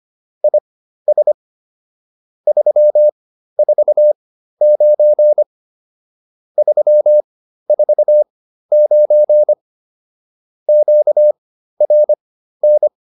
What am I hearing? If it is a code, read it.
Morse code, heard as a keyed tone, about 25 wpm, spaced as for 13 wpm, 600 Hz. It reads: IS 349 349 QRN